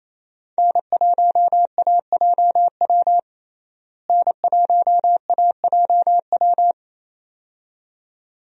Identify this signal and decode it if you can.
Morse code, heard as a keyed tone, about 28 wpm, 700 Hz. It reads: N1AJW N1AJW
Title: USA Call Signs - 5 Characters - Repeated 2x - 28wpm